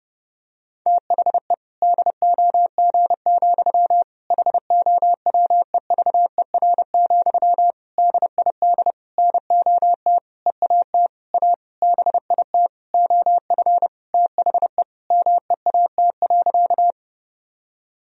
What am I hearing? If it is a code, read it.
Morse code, heard as a keyed tone, about 30 wpm, 700 Hz. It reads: THE DOG, HOWEVER, DID NOT EAT A BIT OF THE MEAT.
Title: Sentences from Top 1000 Words - Review Part 3 - 30wpm